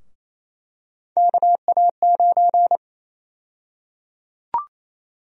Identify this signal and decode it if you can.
Morse code, heard as a keyed tone, about 28 wpm, 700 Hz. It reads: KA9
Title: USA Call Sign Prefixes - 28wpm